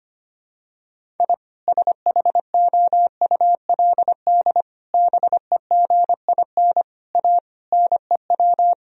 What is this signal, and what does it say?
Morse code, heard as a keyed tone, about 25 wpm, 700 Hz. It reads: I SHOULD BEGIN A NEW
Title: Sentences from Top 1000 Words - Review Part 1 - 25wpm